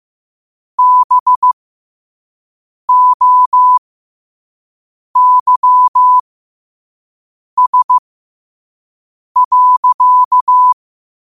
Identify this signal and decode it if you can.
Morse code, heard as a keyed tone, about 15 wpm, spaced as for 5 wpm, 1000 Hz. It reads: BOYS.
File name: qotdfiles / 5wpm